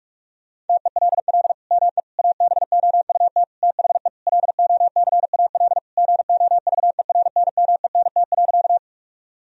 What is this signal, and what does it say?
Morse code, heard as a keyed tone, about 46 wpm, 700 Hz. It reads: TELL ME ABOUT THE LOCAL GOVERNMENT.
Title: Sentences from Top 1000 Words - Review Part 3 - 45wpm